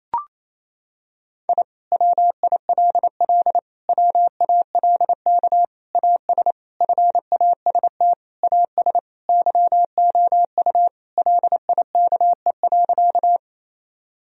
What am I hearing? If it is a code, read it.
Morse code, heard as a keyed tone, about 28 wpm, 700 Hz. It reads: I WILL WALK AS FAST AS YOU LIKE.
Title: Sentences from Top 1000 Words - Review Part 2 - 28wpm